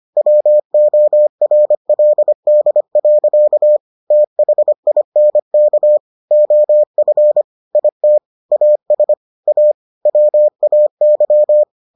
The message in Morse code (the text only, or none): WORLD. THINK OF IT AS A WAY